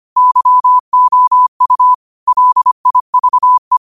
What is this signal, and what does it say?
Morse code, heard as a keyed tone, about 25 wpm, 1000 Hz. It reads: YOU LIVE
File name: 25wpm